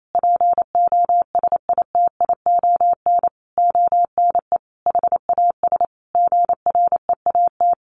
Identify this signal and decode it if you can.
Morse code, heard as a keyed tone, about 28 wpm, 700 Hz. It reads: POSITION ONE HAS GREAT
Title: Sentences from Top 1100 Words 28wpm